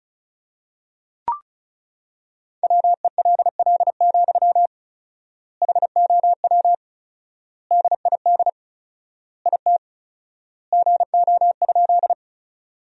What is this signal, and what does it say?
Morse code, heard as a keyed tone, about 36 wpm, 700 Hz. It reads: WELL, HOW DID IT GO?
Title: 4x Word-Spacing - Sentences from Top 200 Words - 35wpm